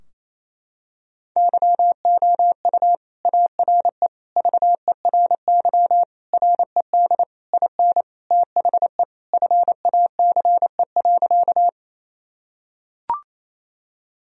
Morse code, heard as a keyed tone, about 27 wpm, 700 Hz. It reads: YOU ARE VERY RED IN THE FACE.
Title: Sentences from Top 1000 Words - Review Part 2 - 28wpm